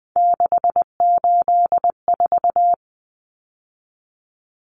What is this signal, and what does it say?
Morse code, heard as a keyed tone, about 20 wpm, 700 Hz. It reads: 684
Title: Numbers - 3-Digit 20wpm